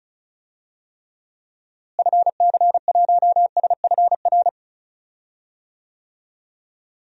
Morse code, heard as a keyed tone, about 36 wpm, 700 Hz. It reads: FC1SFR
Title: DXCC-100 Call Signs - 6 Characters - Repeated 3x - 35wpm